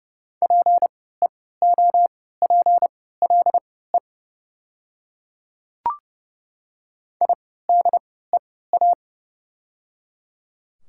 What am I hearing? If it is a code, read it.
Morse code, heard as a keyed tone, about 30 wpm, spaced as for 17 wpm, 700 Hz. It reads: PEOPLE IDEA
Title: 3x Character Spacing - Top 300 Words - 30wpm